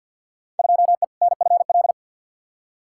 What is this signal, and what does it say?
Morse code, heard as a keyed tone, about 51 wpm, 700 Hz. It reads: 2E NFL
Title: Field Day Exchanges Encore - 50wpm